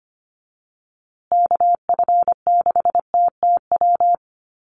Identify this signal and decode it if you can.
Morse code, heard as a keyed tone, about 25 wpm, 700 Hz. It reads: KF6TTW
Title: USA Call Signs Repeated 3x 25wpm